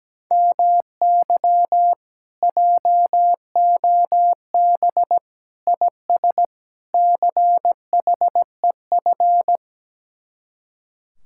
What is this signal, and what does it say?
Morse code, heard as a keyed tone, about 17 wpm, 700 Hz. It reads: MY JOB IS CHEF
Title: Speed-Racing Common QSO Phrases - 17wpm